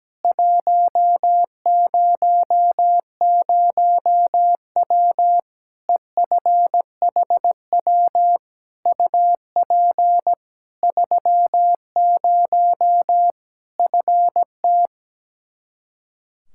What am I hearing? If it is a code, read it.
Morse code, heard as a keyed tone, about 17 wpm, 700 Hz. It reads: 100W EFHW UP 30 FT
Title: K1USN SST QSO Practice 17wpm